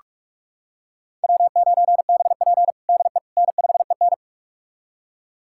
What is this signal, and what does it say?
Morse code, heard as a keyed tone, about 46 wpm, 700 Hz. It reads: W9BP DE N5EN
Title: USA Callsign DE Callsign - 4 Characters - 45wpm